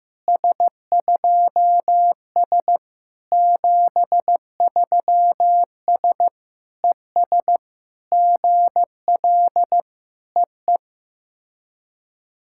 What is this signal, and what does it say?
Morse code, heard as a keyed tone, about 15 wpm, 700 Hz. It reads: S2S 73S ES GL EE
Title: SOTA QSO Practice 15wpm